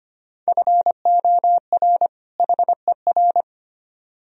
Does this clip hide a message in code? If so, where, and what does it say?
Yes: Morse code from 0.5 to 3.4 s, heard as a keyed tone, about 33 wpm, 700 Hz. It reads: FOR HER